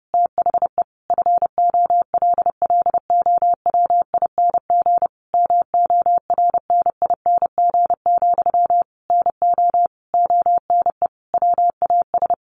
Morse code, heard as a keyed tone, about 30 wpm, 700 Hz. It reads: THE FOLLOWING MORNING, NO ONE WAS
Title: Sentences from Top 1400 Words 30wpm